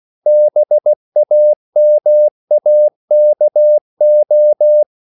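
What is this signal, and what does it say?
Morse code, heard as a keyed tone, about 16 wpm, 600 Hz. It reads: BAMAKO
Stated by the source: CW practice of national capitals at 16 words per minute, from ON6ZQ.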